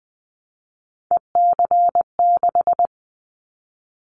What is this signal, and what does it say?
Morse code, heard as a keyed tone, about 20 wpm, 700 Hz. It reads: EC6
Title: Callsign Prefixes for Top 50 DXCC - 20wpm